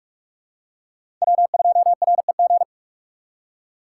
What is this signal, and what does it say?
Morse code, heard as a keyed tone, about 46 wpm, 700 Hz. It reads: W2REG
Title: USA Call Signs - 3x Character Spacing - 45wpm